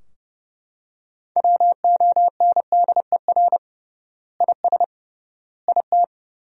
Morse code, heard as a keyed tone, about 30 wpm, 700 Hz. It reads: WONDER IS IT
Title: 3x Word-Spacing - Sentences from Top 500 Words - 30wpm